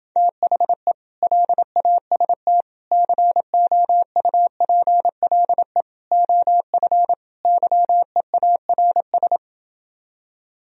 Morse code, heard as a keyed tone, about 28 wpm, 700 Hz. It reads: THE LAST COUPLE OF YEARS